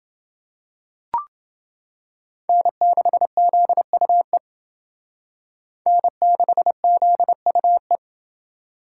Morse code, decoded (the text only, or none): N6ZUE N6ZUE